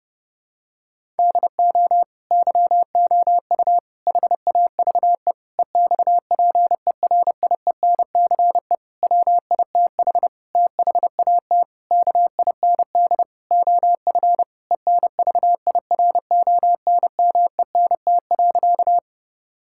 Morse code, heard as a keyed tone, about 30 wpm, 700 Hz. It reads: DO YOU HAVE EXPERIENCE WITH THAT KIND OF ENVIRONMENT.
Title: Sentences from Top 1000 Words - Review Part 5 - 30wpm